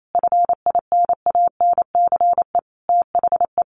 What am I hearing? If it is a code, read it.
Morse code, heard as a keyed tone, about 28 wpm, 700 Hz. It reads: FINANCE THE
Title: Sentences from Top 1500 Words 28wpm